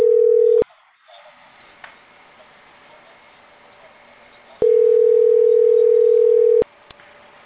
Ringback tone (North America), 440 + 480 Hz.